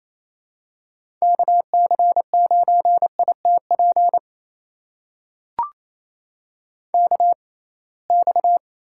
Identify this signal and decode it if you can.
Morse code, heard as a keyed tone, about 28 wpm, spaced as for 21 wpm, 700 Hz. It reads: KC9ITP KX
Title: USA Call Signs - 6x Character Spacing - 28wpm